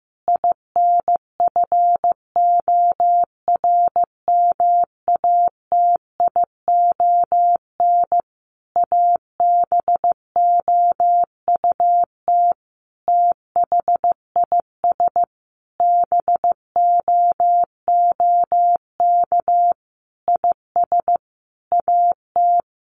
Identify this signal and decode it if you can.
Morse code, heard as a keyed tone, about 15 wpm, 700 Hz. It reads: INFORMATION ABOUT THIS BOOK IS AT
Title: Sentences from Top 1200 Words 15wpm